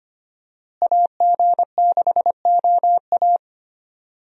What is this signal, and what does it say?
Morse code, heard as a keyed tone, about 25 wpm, 700 Hz. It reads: AG6OA
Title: USA Call Signs - 5 Characters - Repeated 2x - 25wpm